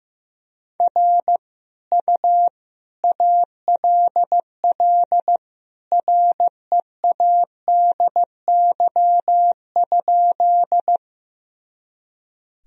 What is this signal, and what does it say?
Morse code, heard as a keyed tone, about 16 wpm, 700 Hz. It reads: R U ALL READY?
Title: Sentences from Top 400 Words and Abbreviations 15wpm